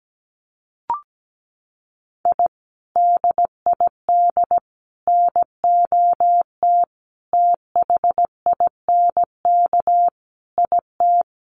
Morse code, heard as a keyed tone, about 17 wpm, 700 Hz. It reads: I DID NOT THINK IT
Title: Sets of 5 Words Encore2 17wpm